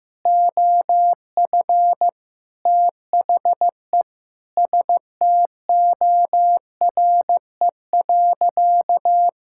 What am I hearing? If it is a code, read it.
Morse code, heard as a keyed tone, about 15 wpm, 700 Hz. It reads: OF THE STORE.
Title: Sentences from Top 1000 Words - Review Part 5 - 15wpm